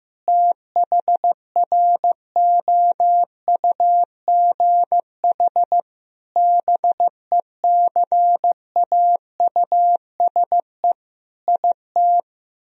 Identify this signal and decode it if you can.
Morse code, heard as a keyed tone, about 15 wpm, 700 Hz. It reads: THROUGH BECAUSE IT
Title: Sentences from Top 1000 Words - Review Part 1 - 15wpm